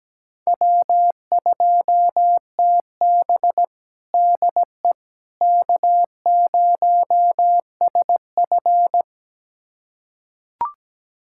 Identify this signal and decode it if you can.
Morse code, heard as a keyed tone, about 17 wpm, 700 Hz. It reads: W2TB DE K0SF
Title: USA Callsign DE Callsign - 4 Characters - 17wpm